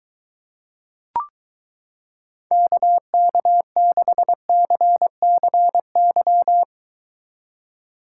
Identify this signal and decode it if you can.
Morse code, heard as a keyed tone, about 23 wpm, 700 Hz. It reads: KK6CCY